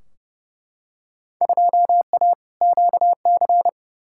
Morse code, heard as a keyed tone, about 32 wpm, 700 Hz. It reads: 2A QC
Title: Field Day Exchanges - 30wpm